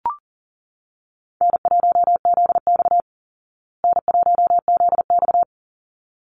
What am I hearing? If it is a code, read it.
Morse code, heard as a keyed tone, about 40 wpm, 700 Hz. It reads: N1ZX N1ZX
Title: Easy Call Signs Repeated 2x - 40wpm